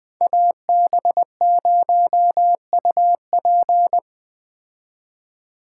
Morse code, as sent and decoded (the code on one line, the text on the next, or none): .- -... ----- ..- .--.
AB0UP